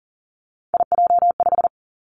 Morse code, heard as a keyed tone, about 42 wpm, 700 Hz. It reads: IJ5
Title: Three Challenging Letter-Number Combinations 40wpm